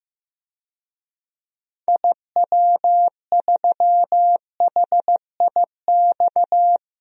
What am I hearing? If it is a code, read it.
Morse code, heard as a keyed tone, about 16 wpm, 700 Hz. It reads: IW3HIX
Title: DXCC-100 Call Signs - 6 Characters - 15wpm